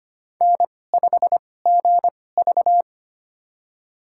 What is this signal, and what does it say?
Morse code, heard as a keyed tone, about 25 wpm, spaced as for 18 wpm, 700 Hz. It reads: N5GV
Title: USA Call Signs - 4 Characters - 2x Character Spacing - 25wpm